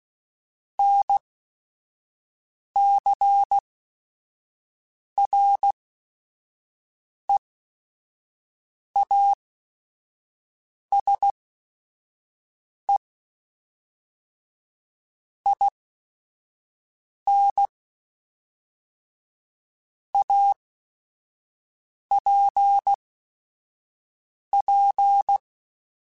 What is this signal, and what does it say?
Morse code, heard as a keyed tone, about 16 wpm, spaced as for 5 wpm, 800 Hz. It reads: NCREASE IN APP
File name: Technical Subjects / Technical Subjects-mp3s / XC @5